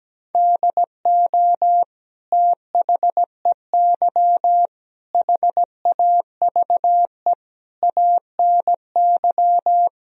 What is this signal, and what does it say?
Morse code, heard as a keyed tone, about 17 wpm, 700 Hz. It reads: DO THEY HAVE ANY
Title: Sentences from Top 1000 Words - Review Part 3 - 17wpm